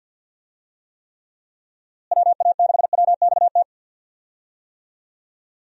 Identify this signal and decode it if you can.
Morse code, heard as a keyed tone, about 51 wpm, 700 Hz. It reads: WA6WXT
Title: USA Call Signs - 6 Characters - Repeated 2x - 50wpm